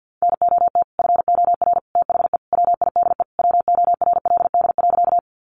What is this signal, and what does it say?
Morse code, heard as a keyed tone, about 50 wpm, 700 Hz. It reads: NOT FOR THE WIDE WORLD.
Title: Sentences from Top 900 Words 50wpm